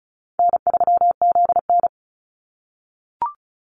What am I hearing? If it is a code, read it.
Morse code, heard as a keyed tone, about 36 wpm, 700 Hz. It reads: N3ZN
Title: Call Signs - 35wpm